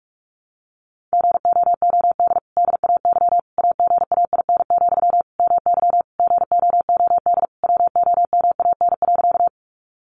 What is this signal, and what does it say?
Morse code, heard as a keyed tone, about 44 wpm, 700 Hz. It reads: GOOD DAY AGAIN, MY GOOD WOMAN.
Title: Sentences from Top 1100 Words 45wpm